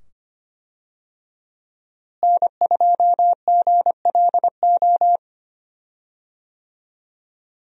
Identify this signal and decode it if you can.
Morse code, heard as a keyed tone, about 25 wpm, 700 Hz. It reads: N2GLO
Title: USA Call Signs - 5 Characters - Repeated 2x - 25wpm